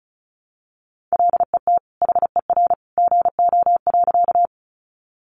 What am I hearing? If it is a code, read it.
Morse code, heard as a keyed tone, about 34 wpm, 700 Hz. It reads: LET HER GO.
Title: Sentences from Top 300 Words 35wpm